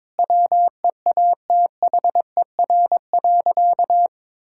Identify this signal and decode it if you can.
Morse code, heard as a keyed tone, about 22 wpm, 700 Hz. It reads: WEATHER.
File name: Sentences with 9-Word Phrases - 22wpm